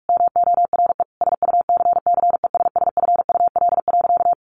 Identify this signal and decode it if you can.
Morse code, heard as a keyed tone, about 44 wpm, 700 Hz. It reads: MORE SUCCESSFUL.
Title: Sentences from Top 1500 Words 45wpm